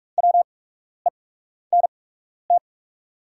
Morse code, heard as a keyed tone, about 44 wpm, spaced as for 12 wpm, 700 Hz. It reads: WENT
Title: 8x Character Spacing - Top 500 Words - 45wpm